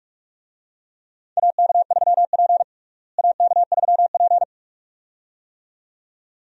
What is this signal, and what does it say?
Morse code, heard as a keyed tone, about 47 wpm, 700 Hz. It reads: AK3P AK3P
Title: USA Call Signs - 4 Characters - Repeated 2x - 45wpm